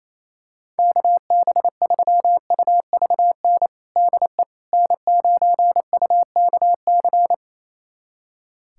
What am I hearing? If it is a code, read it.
Morse code, heard as a keyed tone, about 28 wpm, 700 Hz. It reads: KB3UVN DE N9UKC
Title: USA Callsign DE Callsign 28wpm